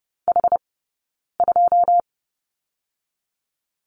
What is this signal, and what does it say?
Morse code, heard as a keyed tone, about 30 wpm, 700 Hz. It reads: H 2
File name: Lesson 16d - Letter c - Two Characters 30wpm